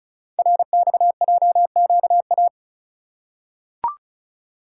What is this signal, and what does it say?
Morse code, heard as a keyed tone, about 36 wpm, 700 Hz. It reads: RXJQA
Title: Code Groups - 5 Letters - 35wpm